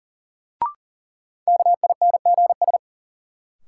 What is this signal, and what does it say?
Morse code, heard as a keyed tone, about 40 wpm, 700 Hz. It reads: KINGS
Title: Sentences from Top 1000 Words - Review Part 6 - New Word Drill - 40wpm